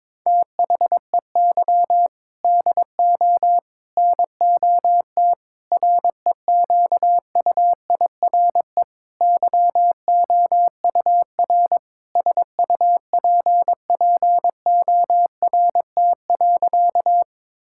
Morse code, heard as a keyed tone, about 22 wpm, 700 Hz. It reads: THEY DO NOT REQUIRE YOUR SUPPORT.
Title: Sentences from Top 1000 Words - Review Part 6 - 22wpm